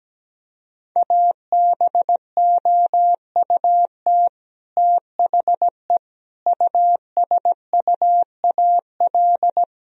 Morse code, heard as a keyed tone, about 17 wpm, 700 Hz. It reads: ABOUT THE USUAL
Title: Sentences from Top 1000 Words - Review Part 2 - 17wpm